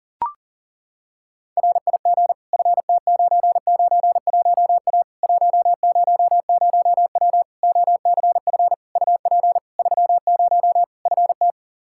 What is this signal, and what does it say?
Morse code, heard as a keyed tone, about 40 wpm, 700 Hz. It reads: RIG FT991A 100W OCF UP 30 FT
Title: Practice QSO - Exchange 2 - Info - 40wpm